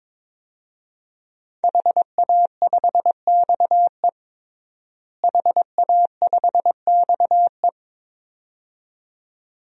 Morse code, heard as a keyed tone, about 22 wpm, 700 Hz. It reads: HA5XE HA5XE
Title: DXCC-100 Call Signs - 5 Characters - Repeated 2x - 22wpm